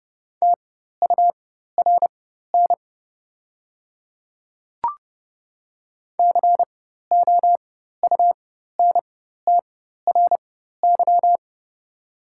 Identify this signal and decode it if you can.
Morse code, heard as a keyed tone, about 30 wpm, spaced as for 14 wpm, 700 Hz. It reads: TURN COUNTRY
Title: 4x Character Spacing - Top 300 Words - 30wpm